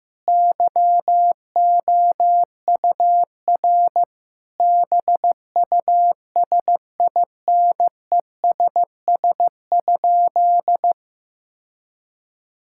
Morse code, heard as a keyed tone, about 15 wpm, 700 Hz. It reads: YOUR BUSINESS?
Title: Sentences from Top 1000 Words - Review Part 6 - 15wpm